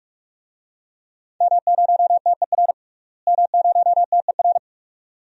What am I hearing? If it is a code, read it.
Morse code, heard as a keyed tone, about 45 wpm, 700 Hz. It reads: M0TER M0TER
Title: DXCC-100 Call Signs - 5 Characters - Repeated 2x - 45wpm